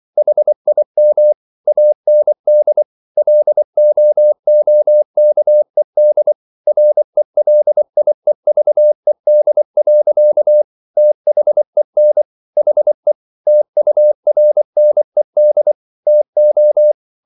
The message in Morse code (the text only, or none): HIM AND LOOKED RELIEVED. THEN HE TURNED TO